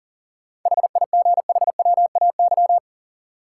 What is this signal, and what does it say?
Morse code, heard as a keyed tone, about 40 wpm, 700 Hz. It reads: HIGHWAY
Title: Sentences from Top 1400 Words - New Word Drill - 40wpm